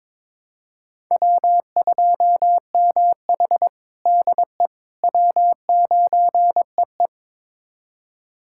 Morse code, heard as a keyed tone, about 23 wpm, 700 Hz. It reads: W2MH DE W9EE
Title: USA Callsign DE Callsign - 4 Characters - 22wpm